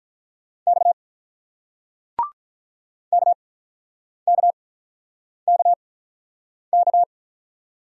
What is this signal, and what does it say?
Morse code, heard as a keyed tone, about 46 wpm, 700 Hz. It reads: X K K K K